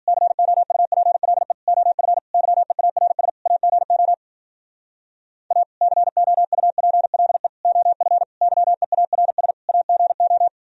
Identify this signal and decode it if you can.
Morse code, heard as a keyed tone, about 52 wpm, 700 Hz. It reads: COUPLE OF YEARS AGO A COUPLE OF YEARS AGO